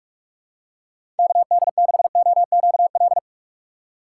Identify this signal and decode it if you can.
Morse code, heard as a keyed tone, about 42 wpm, 700 Hz. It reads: KD6OQL
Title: USA Call Signs - 4x Character Spacing - 45wpm